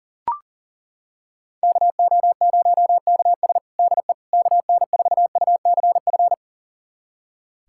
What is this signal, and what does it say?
Morse code, heard as a keyed tone, about 40 wpm, 700 Hz. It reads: KO0KS DE KN4UCF
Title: USA Callsign DE Callsign 40wpm